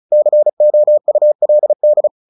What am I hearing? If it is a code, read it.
Morse code, heard as a keyed tone, about 34 wpm, 600 Hz. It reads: COULD
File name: the old man and the sea / the old man and the sea 1 35wpm